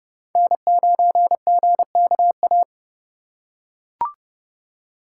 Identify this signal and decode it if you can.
Morse code, heard as a keyed tone, about 29 wpm, 700 Hz. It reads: N9GKA
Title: USA Call Signs - 5 Characters - 4x Character Spacing - 30wpm